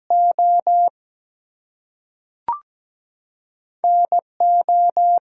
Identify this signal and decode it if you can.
Morse code, heard as a keyed tone, about 17 wpm, spaced as for 3 wpm, 700 Hz. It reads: ON O
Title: Two Letter Words - 17wpm